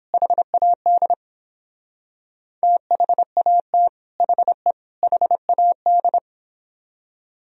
Morse code, heard as a keyed tone, about 29 wpm, 700 Hz. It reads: HAD THAT HE HAD